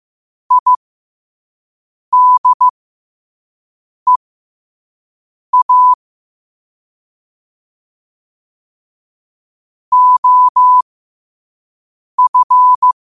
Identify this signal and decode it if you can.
Morse code, heard as a keyed tone, about 15 wpm, spaced as for 5 wpm, 1000 Hz. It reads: IDEA OF
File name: qotdfiles / 5wpm